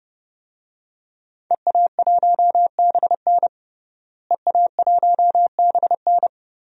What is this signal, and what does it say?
Morse code, heard as a keyed tone, about 30 wpm, 700 Hz. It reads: EA1BN EA1BN
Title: DXCC-100 Call Signs - 5 Characters - Repeated 2x - 30wpm